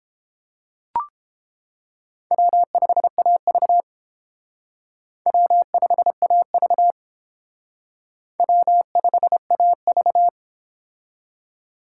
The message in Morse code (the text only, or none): W5AV W5AV W5AV